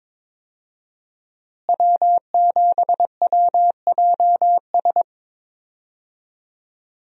Morse code, heard as a keyed tone, about 22 wpm, 700 Hz. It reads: W7WJS
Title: USA Call Signs - 5 Characters - Repeated 3x - 22wpm